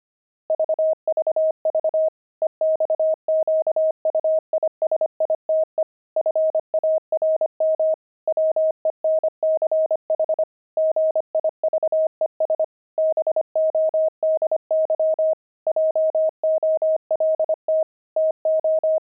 Morse code, heard as a keyed tone, about 25 wpm, 630 Hz. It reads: VVV EXQUISITE FARM WENCH GIVES BODY JOLT TO
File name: pangram-fs-25-cs-25-jolt